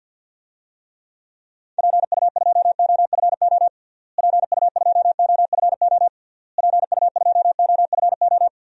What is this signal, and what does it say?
Morse code, heard as a keyed tone, about 50 wpm, 700 Hz. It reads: PU2OFO PU2OFO PU2OFO
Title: DXCC-100 Call Signs - 6 Characters - Repeated 3x - 50wpm